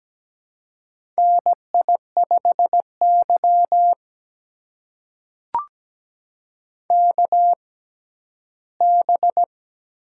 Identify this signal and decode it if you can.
Morse code, heard as a keyed tone, about 17 wpm, spaced as for 12 wpm, 700 Hz. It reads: NI5Y KB
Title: USA Call Signs - 4 Characters - 6x Character Spacing - 17wpm